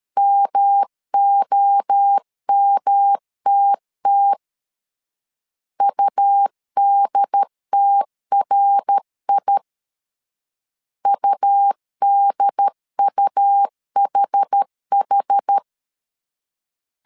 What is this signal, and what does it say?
Morse code, heard as a keyed tone, about 13 wpm, 800 Hz. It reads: MOMTT UDTRI UDUHH
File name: CW / DET / 1 3 1500C S A 12wpm 2905